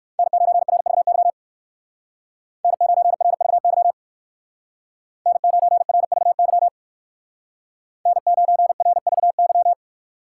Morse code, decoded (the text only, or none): N9RVY N9RVY N9RVY N9RVY